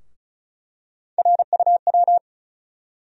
Morse code, heard as a keyed tone, about 34 wpm, 700 Hz. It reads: RUW
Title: USA Call Sign Postfixes - 3 Letters 35wpm